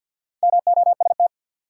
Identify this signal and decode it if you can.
Morse code, heard as a keyed tone, about 54 wpm, 700 Hz. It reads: MOST
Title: Top 100 Words 50wpm